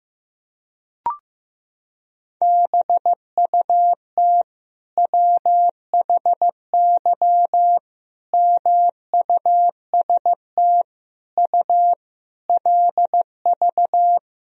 Morse code, heard as a keyed tone, about 16 wpm, 700 Hz. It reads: BUT WHY MUST U LV
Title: Sentences from Top 400 Words and Abbreviations 15wpm